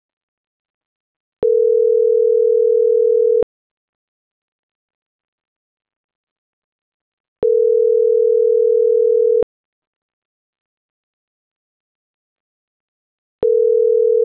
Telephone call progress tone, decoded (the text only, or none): ringback tone